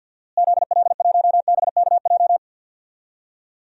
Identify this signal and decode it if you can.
Morse code, heard as a keyed tone, about 52 wpm, 700 Hz. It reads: ZL1BKJ
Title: DXCC-100 Call Signs - 6 Characters - Repeated 2x - 50wpm